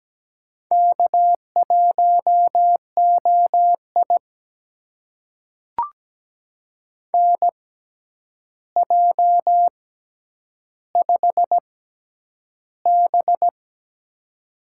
Morse code, heard as a keyed tone, about 17 wpm, spaced as for 9 wpm, 700 Hz. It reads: K1OI NJ5B
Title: USA Call Signs - 4 Characters - 6x Character Spacing - 17wpm